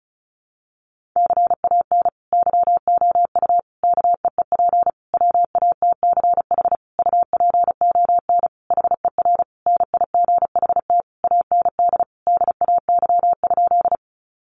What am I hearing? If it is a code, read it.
Morse code, heard as a keyed tone, about 34 wpm, 700 Hz. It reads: CAN YOU KEEP WATCH UPON HER NIGHT AND DAY?
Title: Sentences from Top 1100 Words 35wpm